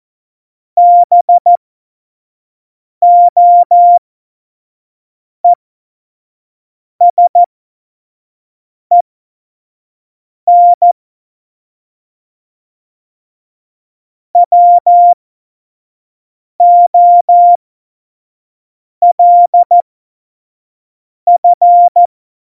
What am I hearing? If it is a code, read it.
Morse code, heard as a keyed tone, about 14 wpm, spaced as for 5 wpm, 700 Hz. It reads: BOESEN WOLF
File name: Morsekurs / Witze / Witz 09. Januar 2018, 25 F70